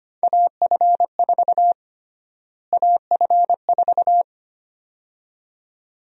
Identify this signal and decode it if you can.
Morse code, heard as a keyed tone, about 25 wpm, 700 Hz. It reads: AF4 AF4
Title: USA Call Sign Prefixes - Repeated 2x - 25wpm